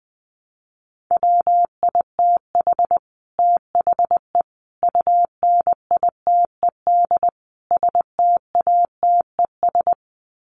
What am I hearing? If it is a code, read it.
Morse code, heard as a keyed tone, about 20 wpm, 700 Hz. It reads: WITH THE UNITED STATES
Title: Sets of 4 Words 20wpm